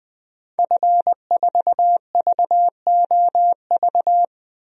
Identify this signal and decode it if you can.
Morse code, heard as a keyed tone, about 20 wpm, 700 Hz. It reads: F4VOV
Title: DXCC-100 Call Signs - 5 Characters - 20wpm